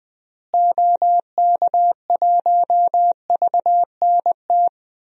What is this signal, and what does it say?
Morse code, heard as a keyed tone, about 20 wpm, 700 Hz. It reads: OK1VNT
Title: DXCC-100 Call Signs - 6 Characters - Repeated 2x - 20wpm